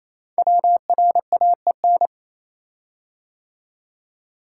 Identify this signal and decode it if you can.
Morse code, heard as a keyed tone, about 27 wpm, 700 Hz. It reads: WRAEN
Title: Code Groups - 5 Letters - 28wpm